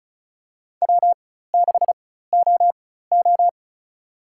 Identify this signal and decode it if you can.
Morse code, heard as a keyed tone, about 36 wpm, spaced as for 16 wpm, 700 Hz. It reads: W6OO
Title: USA Call Signs - 4 Characters - 4x Character Spacing - 35wpm